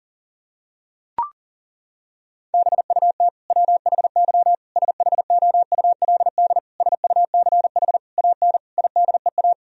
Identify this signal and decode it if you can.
Morse code, heard as a keyed tone, about 40 wpm, 700 Hz. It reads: BUT WHY SHOULD SUCH AN IDEA
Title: Sentences from Top 1000 Words - Review Part 1 - 40wpm